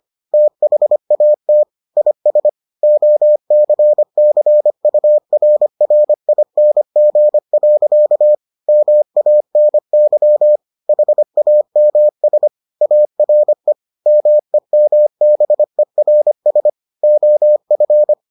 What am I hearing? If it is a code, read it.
Morse code, heard as a keyed tone, about 25 wpm, 600 Hz. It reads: THAT IS OCCURRING. MANY HAMS ARE MEMBERS OF